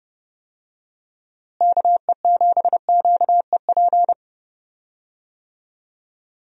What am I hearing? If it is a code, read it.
Morse code, heard as a keyed tone, about 31 wpm, 700 Hz. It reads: KE7QEP
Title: USA Call Signs - 6 Characters - Repeated 3x - 30wpm